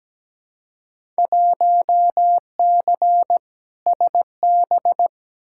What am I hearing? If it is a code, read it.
Morse code, heard as a keyed tone, about 17 wpm, 700 Hz. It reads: 1C SB
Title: Field Day Exchanges Encore - 17wpm